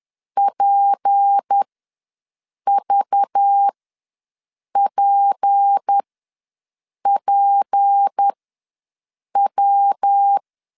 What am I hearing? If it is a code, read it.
Morse code, heard as a keyed tone, about 10 wpm, spaced as for 6 wpm, 800 Hz. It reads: PVPPW
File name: LEZIONI / 6 PKX0 / 6 P ws 10wpm 3751